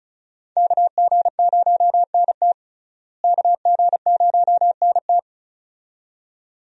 Morse code, heard as a keyed tone, about 34 wpm, 700 Hz. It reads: KG0NT KG0NT
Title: USA Call Signs - 5 Characters - Repeated 2x - 35wpm